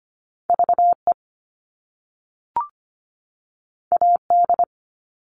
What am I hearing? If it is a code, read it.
Morse code, heard as a keyed tone, about 25 wpm, 700 Hz. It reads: VE AD
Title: All Cut Numbers - 2-Digit 25wpm